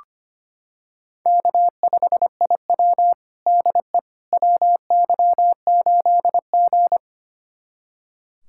Morse code, heard as a keyed tone, about 25 wpm, 700 Hz. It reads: K5IW DE WY8G
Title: USA Callsign DE Callsign - 4 Characters - 25wpm